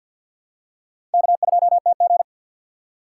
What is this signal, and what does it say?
Morse code, heard as a keyed tone, about 51 wpm, 700 Hz. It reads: K2TG
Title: USA Call Signs - 4 Characters - 2x Character Spacing - 50wpm